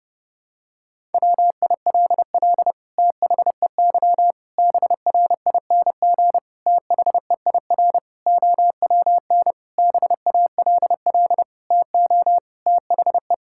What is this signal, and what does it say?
Morse code, heard as a keyed tone, about 30 wpm, 700 Hz. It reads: WILL THEY BRING THEIR OWN BALL TO THE
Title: Sentences from Top 1000 Words - Review Part 6 - 30wpm